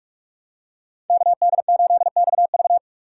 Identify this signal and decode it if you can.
Morse code, heard as a keyed tone, about 42 wpm, 700 Hz. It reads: KD8XV
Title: USA Call Signs - 2x Character Spacing - 45wpm